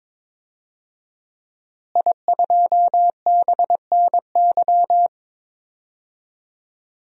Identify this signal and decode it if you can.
Morse code, heard as a keyed tone, about 22 wpm, 700 Hz. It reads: I2BNY
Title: DXCC-100 Call Signs - 5 Characters - Repeated 3x - 22wpm